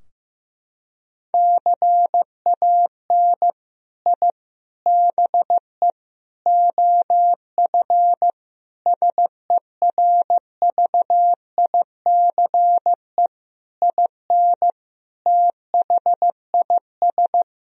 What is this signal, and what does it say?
Morse code, heard as a keyed tone, about 16 wpm, 700 Hz. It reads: CAN I BE OF SERVICE IN THIS
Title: Sentences from Top 1000 Words - Review Part 4 - 15wpm